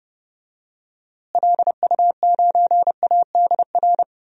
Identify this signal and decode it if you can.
Morse code, heard as a keyed tone, about 32 wpm, 700 Hz. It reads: LU9ADR
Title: DXCC-100 Call Signs - 6 Characters - Repeated 2x - 30wpm